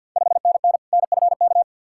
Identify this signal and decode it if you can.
Morse code, heard as a keyed tone, about 51 wpm, 690 Hz. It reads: 5NN NFK
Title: Ontario QSO Party Exchange 50wpm